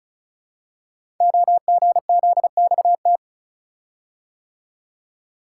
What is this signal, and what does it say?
Morse code, heard as a keyed tone, about 36 wpm, 700 Hz. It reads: OGZXT
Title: Code Groups - 5 Letters - 35wpm